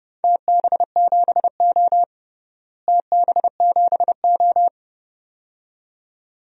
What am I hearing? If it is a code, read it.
Morse code, heard as a keyed tone, about 32 wpm, 700 Hz. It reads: TB7O TB7O
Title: DXCC-100 Call Signs - 4 Characters - Repeated 2x - 30wpm